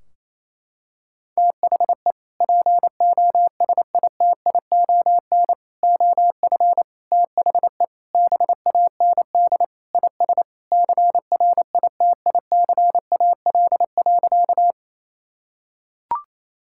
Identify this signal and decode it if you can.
Morse code, heard as a keyed tone, about 28 wpm, 700 Hz. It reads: THE POSITION OF THE BAND IS CRITICAL.
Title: Sentences from Top 1600 Words 28wpm